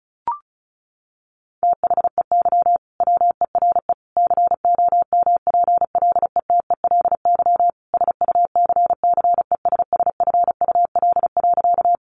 Morse code, heard as a keyed tone, about 34 wpm, 700 Hz. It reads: THEY WERE COMPLETELY SUCCESSFUL.
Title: Sentences from Top 1100 Words 35wpm